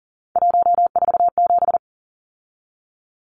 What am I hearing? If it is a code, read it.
Morse code, heard as a keyed tone, about 40 wpm, 700 Hz. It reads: 147
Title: Numbers - 3-Digit 40wpm